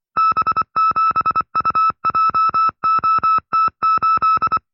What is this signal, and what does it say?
Morse code, heard as a keyed tone, about 24 wpm, 1310 Hz. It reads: B7UJOT8
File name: FreeFiles / CW-practice / 20WPM-side2